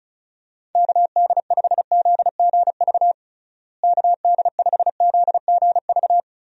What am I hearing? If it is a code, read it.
Morse code, heard as a keyed tone, about 36 wpm, 700 Hz. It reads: KD5ZGV KD5ZGV
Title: USA Call Signs - 6 Characters - Repeated 2x - 35wpm